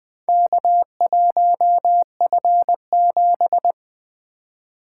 Morse code, heard as a keyed tone, about 20 wpm, 700 Hz. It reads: K1F7
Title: Four Letter-Number Combinations - 20wpm